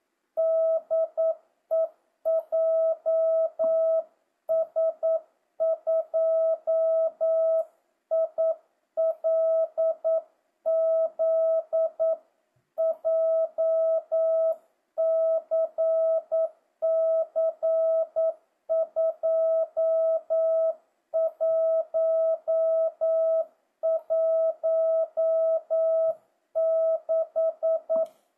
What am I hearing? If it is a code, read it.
Morse code, heard as a keyed tone, about 9 wpm, 650 Hz. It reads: DE JS2ILZ JCC 2116
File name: jh8cbh / 2025cw